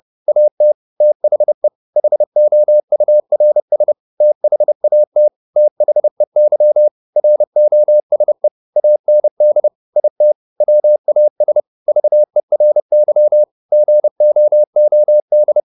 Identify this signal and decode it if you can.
Morse code, heard as a keyed tone, about 30 wpm, 600 Hz. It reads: AT THE HOURS THAT THEY ROSE AND IT WAS VERY GOOD
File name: the old man and the sea / the old man and the sea 1 30wpm